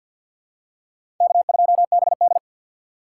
Morse code, heard as a keyed tone, about 51 wpm, 700 Hz. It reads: K2BD
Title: USA Call Signs - 4 Characters - 3x Character Spacing - 50wpm